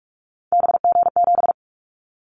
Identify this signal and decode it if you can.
Morse code, heard as a keyed tone, about 46 wpm, 700 Hz. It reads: BG7
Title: Callsign Prefixes for Top 100 DXCC - 45wpm